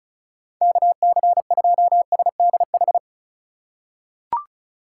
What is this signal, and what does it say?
Morse code, heard as a keyed tone, about 34 wpm, 690 Hz. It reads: KC2SDH
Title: USA Call Signs - 6 Characters - 4x Character Spacing - 35wpm